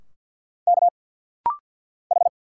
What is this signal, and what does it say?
Morse code, heard as a keyed tone, about 49 wpm, 700 Hz. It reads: K H
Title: Single Letters - Rapid-Fire 50wpm